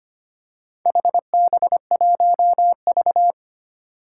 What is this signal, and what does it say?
Morse code, heard as a keyed tone, about 25 wpm, 700 Hz. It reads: HB1V
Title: Four Challenging Letter-Number Combinations 25wpm